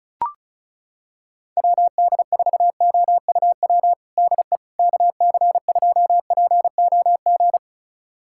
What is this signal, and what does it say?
Morse code, heard as a keyed tone, about 36 wpm, 700 Hz. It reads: WD4OUW DE KC2POG
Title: USA Callsign DE Callsign 35wpm